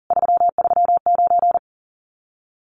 Morse code, heard as a keyed tone, about 37 wpm, 700 Hz. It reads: 339
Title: Common QSO Phrases Encore 40wpm